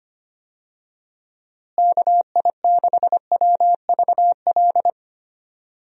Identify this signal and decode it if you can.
Morse code, heard as a keyed tone, about 25 wpm, 700 Hz. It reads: KI6WVL
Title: USA Call Signs - 6 Characters - Repeated 3x - 25wpm